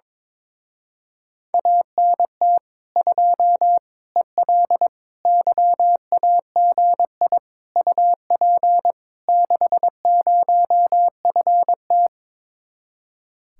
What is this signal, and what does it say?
Morse code, heard as a keyed tone, about 22 wpm, 700 Hz. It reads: ANT 2 EL YAGI UP 60FT
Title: Speed-Racing Common QSO Phrases - 22wpm